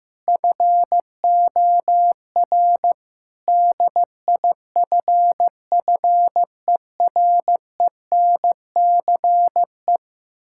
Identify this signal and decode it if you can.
Morse code, heard as a keyed tone, about 15 wpm, 700 Hz. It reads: FOR DIFFERENCE
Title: Sentences from Top 1600 Words 15wpm